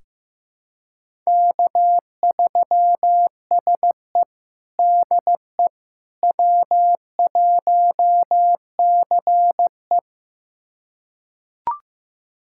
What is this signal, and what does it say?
Morse code, heard as a keyed tone, about 15 wpm, 700 Hz. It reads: K3SE DE W1CE
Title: USA Callsign DE Callsign - 4 Characters - 15wpm